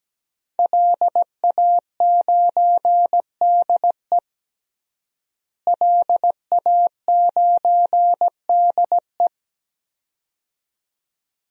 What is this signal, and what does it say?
Morse code, heard as a keyed tone, about 17 wpm, 700 Hz. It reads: LA9DE LA9DE
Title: DXCC-100 Call Signs - 5 Characters - Repeated 2x - 17wpm